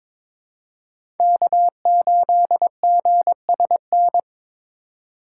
Morse code, heard as a keyed tone, about 22 wpm, 700 Hz. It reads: K8GSN
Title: USA Call Signs - 5 Characters - 6x Character Spacing - 22wpm